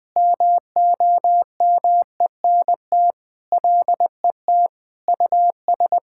Morse code, heard as a keyed tone, about 21 wpm, 700 Hz. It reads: MOMENT LET US
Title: Sentences with 4-Word Phrases - 20wpm